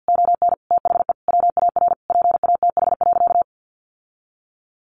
Morse code, heard as a keyed tone, about 48 wpm, 700 Hz. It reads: ON THE WAR PATH.
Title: Sentences from Top 1400 Words 50wpm